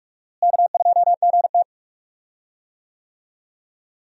Morse code, heard as a keyed tone, about 46 wpm, 700 Hz. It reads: K2GT
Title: USA Call Signs - 4 Characters - Repeated 3x - 45wpm